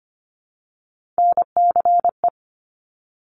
Morse code, heard as a keyed tone, about 25 wpm, 700 Hz. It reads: NCE
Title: Top Three Letter Combinations in Words 25wpm